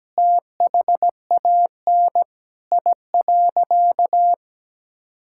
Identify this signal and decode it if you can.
Morse code, heard as a keyed tone, about 17 wpm, 700 Hz. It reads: THAN I.
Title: Sentences from Top 1000 Words - Review Part 5 - 17wpm